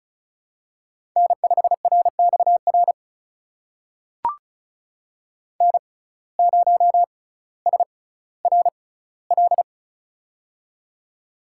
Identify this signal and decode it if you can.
Morse code, heard as a keyed tone, about 36 wpm, spaced as for 18 wpm, 700 Hz. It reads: N5RXR N0SRL
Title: USA Call Signs - 5 Characters - 6x Character Spacing - 35wpm